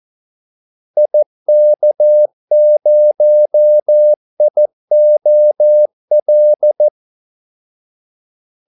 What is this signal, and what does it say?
Morse code, heard as a keyed tone, about 14 wpm, 600 Hz. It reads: IK0IOL
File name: Callsigns-14wpm